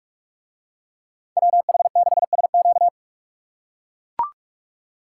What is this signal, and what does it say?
Morse code, heard as a keyed tone, about 41 wpm, 700 Hz. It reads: WH6SQ